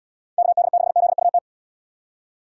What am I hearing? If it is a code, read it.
Morse code, heard as a keyed tone, about 78 wpm, 700 Hz. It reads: KD6ZUT